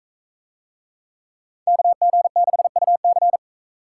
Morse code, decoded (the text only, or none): KG6UC